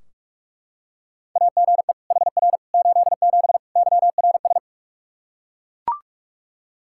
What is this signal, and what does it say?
Morse code, heard as a keyed tone, about 41 wpm, 700 Hz. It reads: AGE HR 87 YRS
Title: Practice QSO - Exchange 3 - More Info - 45wpm